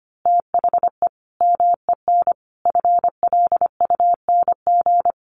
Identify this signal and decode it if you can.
Morse code, heard as a keyed tone, about 25 wpm, 700 Hz. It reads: THE MEN FLUNG
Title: A Sound Of Thunder 25wpm